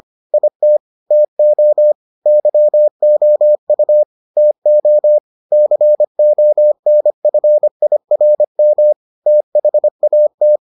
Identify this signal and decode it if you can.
Morse code, heard as a keyed tone, about 25 wpm, 600 Hz. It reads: IT TO YOU TO CONFIRM THAT